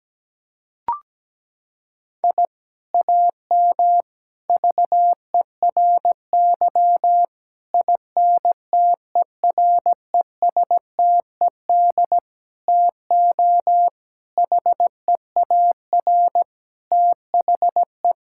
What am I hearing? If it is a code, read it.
Morse code, heard as a keyed tone, about 17 wpm, 700 Hz. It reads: I AM VERY INTERESTED TO HEAR THE
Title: Sentences from Top 1000 Words - Review Part 6 - 17wpm